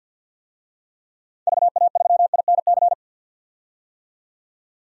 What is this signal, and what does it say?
Morse code, heard as a keyed tone, about 47 wpm, 700 Hz. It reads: VA3INC
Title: DXCC-100 Call Signs - 6 Characters - 50wpm